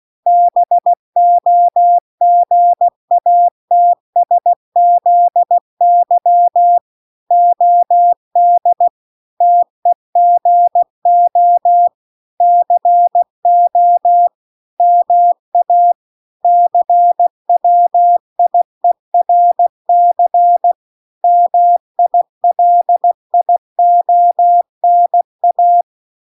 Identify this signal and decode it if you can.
Morse code, heard as a keyed tone, about 16 wpm, 700 Hz. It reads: BOGATSZY OD TEGO CO MA CWIERC MILIONA